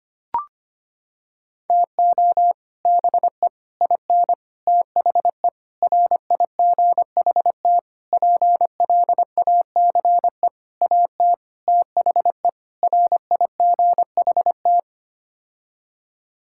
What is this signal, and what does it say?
Morse code, heard as a keyed tone, about 25 wpm, 700 Hz. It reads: TO BE IN THE RIGHT PLACE AT THE RIGHT
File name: Sets of 9 Words - 25wpm